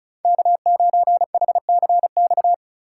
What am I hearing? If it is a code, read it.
Morse code, heard as a keyed tone, about 36 wpm, 700 Hz. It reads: K9HCX
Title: USA Call Signs - 5 Characters - 4x Character Spacing - 35wpm